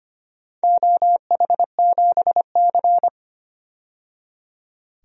Morse code, heard as a keyed tone, about 25 wpm, 700 Hz. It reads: OH7C
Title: DXCC-100 Call Signs - 4 Characters - 25wpm